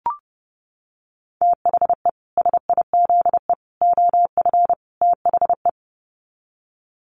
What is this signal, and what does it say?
Morse code, heard as a keyed tone, about 28 wpm, 700 Hz. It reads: THE SIZE OF THE
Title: Sets of 4 Words 30wpm